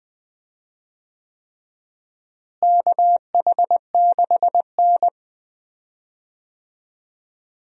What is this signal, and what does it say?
Morse code, heard as a keyed tone, about 20 wpm, 700 Hz. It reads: KH6N
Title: DXCC-100 Call Signs - 4 Characters - Repeated 3x - 20wpm